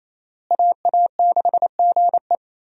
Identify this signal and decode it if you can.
Morse code, heard as a keyed tone, about 28 wpm, 700 Hz. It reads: AA6GE
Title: USA Call Signs - 5 Characters - Repeated 3x - 28wpm